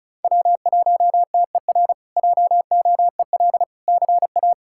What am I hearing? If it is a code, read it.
Morse code, heard as a keyed tone, about 36 wpm, 700 Hz. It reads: W1TER JOEL CA
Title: K1USN SST QSO Practice - 35wpm